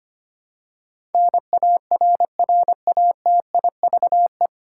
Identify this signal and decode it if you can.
Morse code, heard as a keyed tone, about 25 wpm, 700 Hz. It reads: NARRATIVE
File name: practice / 25wpm / 9 Letter Words - First 3 Letters Given - 25wpm